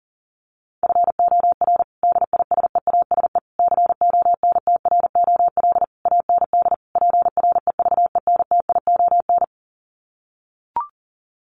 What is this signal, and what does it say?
Morse code, heard as a keyed tone, about 40 wpm, 700 Hz. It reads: FOR DISEASE CONTROL AND PREVENTION
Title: Sets of 5 Words Encore2 40wpm